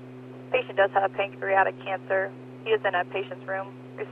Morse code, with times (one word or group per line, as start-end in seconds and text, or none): none